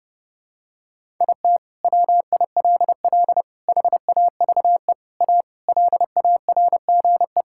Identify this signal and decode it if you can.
Morse code, heard as a keyed tone, about 32 wpm, 700 Hz. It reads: IT WILL HAVE A LARGE
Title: Sentences from Top 1000 Words - Review Part 6 - 30wpm